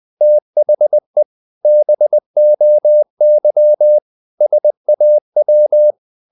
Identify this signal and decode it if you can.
Morse code, heard as a keyed tone, about 20 wpm, 600 Hz. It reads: THE BOY SAW
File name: the old man and the sea part 2 20wpm